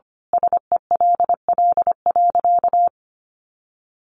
Morse code, heard as a keyed tone, about 25 wpm, 700 Hz. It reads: SELL.
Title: Sentences from Top 1100 Words 25wpm